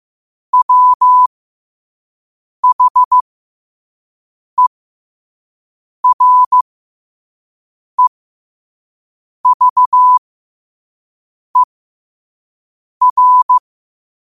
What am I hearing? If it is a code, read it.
Morse code, heard as a keyed tone, about 15 wpm, spaced as for 5 wpm, 1000 Hz. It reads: WHEREVER
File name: qotdfiles / 5wpm